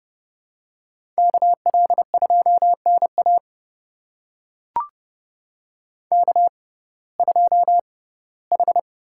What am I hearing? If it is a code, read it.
Morse code, heard as a keyed tone, about 29 wpm, spaced as for 18 wpm, 700 Hz. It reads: KL2NA K2H
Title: USA Call Signs - 5 Characters - 6x Character Spacing - 30wpm